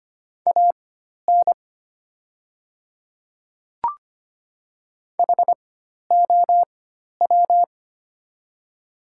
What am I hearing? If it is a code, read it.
Morse code, heard as a keyed tone, about 25 wpm, spaced as for 12 wpm, 700 Hz. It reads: AN HOW
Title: 4x Character Spacing - Top 100 Words - 25wpm